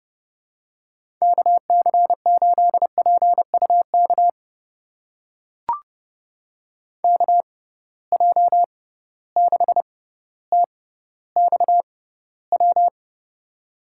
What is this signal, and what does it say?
Morse code, heard as a keyed tone, about 31 wpm, spaced as for 16 wpm, 700 Hz. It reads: KC8PUK KJ6TXW
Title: USA Call Signs - 6x Character Spacing - 30wpm